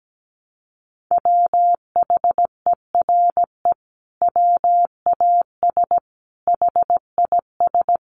Morse code, heard as a keyed tone, about 17 wpm, 700 Hz. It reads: WHERE WAS HIS
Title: Sentences from Top 1200 Words 17wpm